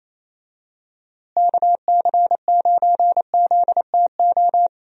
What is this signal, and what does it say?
Morse code, heard as a keyed tone, about 28 wpm, 700 Hz. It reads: KC9ZTO
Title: USA Call Signs - 6 Characters - Repeated 3x - 28wpm